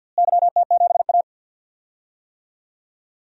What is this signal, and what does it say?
Morse code, heard as a keyed tone, about 48 wpm, 700 Hz. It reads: YT7A
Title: DXCC-100 Call Signs - 4 Characters - Repeated 3x - 50wpm